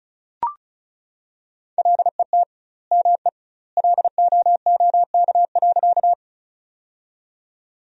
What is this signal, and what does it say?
Morse code, heard as a keyed tone, about 36 wpm, 700 Hz. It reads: LET ME LOOK.
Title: 2x Word-Spacing - Sentences from Top 300 Words - 35wpm